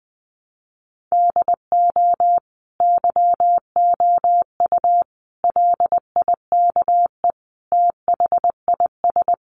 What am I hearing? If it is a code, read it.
Morse code, heard as a keyed tone, about 20 wpm, 700 Hz. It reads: DO YOU LIKE THIS
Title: Sentences from Top 1200 Words 20wpm